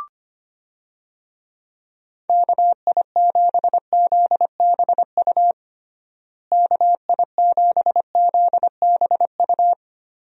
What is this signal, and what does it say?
Morse code, heard as a keyed tone, about 25 wpm, 700 Hz. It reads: KI7ZBU KI7ZBU
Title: USA Call Signs - 6 Characters - Repeated 3x - 25wpm